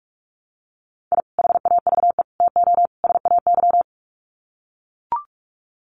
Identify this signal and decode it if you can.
Morse code, heard as a keyed tone, about 44 wpm, 700 Hz. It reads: I HAVE TO SAY
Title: Sets of 4 Words 45wpm